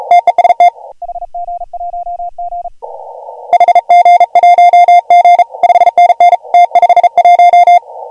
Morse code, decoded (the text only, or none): TEST HG1G HG1G 5NN T51